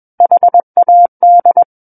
Morse code, heard as a keyed tone, about 21 wpm, 700 Hz. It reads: HAD
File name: Book One / 12-Chapter 12 - 21 WPM